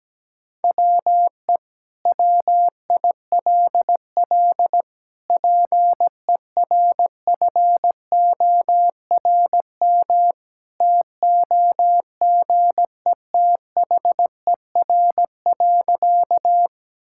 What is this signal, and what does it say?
Morse code, heard as a keyed tone, about 17 wpm, 700 Hz. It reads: WE WILL PERFORM TOGETHER.
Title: Sentences from Top 1000 Words - Review Part 5 - 17wpm